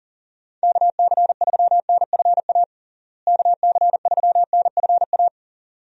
Morse code, heard as a keyed tone, about 40 wpm, 700 Hz. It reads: KC3NFA KC3NFA
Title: USA Call Signs - 6 Characters - Repeated 2x - 40wpm